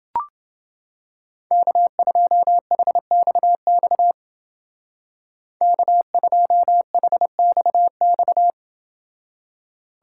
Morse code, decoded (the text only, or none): K2HXX K2HXX